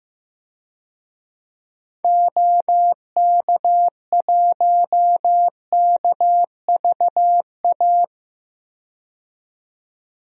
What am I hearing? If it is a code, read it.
Morse code, heard as a keyed tone, about 16 wpm, 700 Hz. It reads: OK1KVA
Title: DXCC-100 Call Signs - 6 Characters - 15wpm